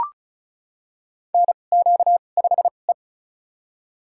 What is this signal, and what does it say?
Morse code, heard as a keyed tone, about 36 wpm, spaced as for 25 wpm, 700 Hz. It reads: NQ5E
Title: USA Call Signs - 4 Characters - 2x Character Spacing - 35wpm